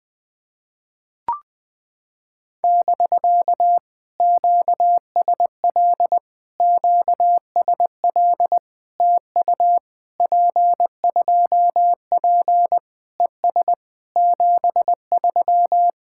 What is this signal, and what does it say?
Morse code, heard as a keyed tone, about 20 wpm, 700 Hz. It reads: <BK> QSL QSL TU P2P ES 73
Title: POTA QSO Practice - 20wpm